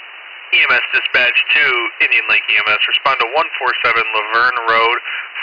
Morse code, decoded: none

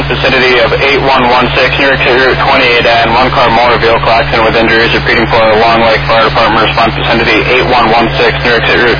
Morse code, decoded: none

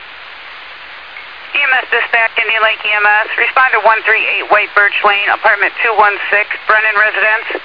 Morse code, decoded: none

